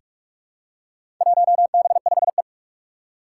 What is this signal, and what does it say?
Morse code, heard as a keyed tone, about 42 wpm, 700 Hz. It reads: 1B5E
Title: Four Challenging Letter-Number Combinations 45wpm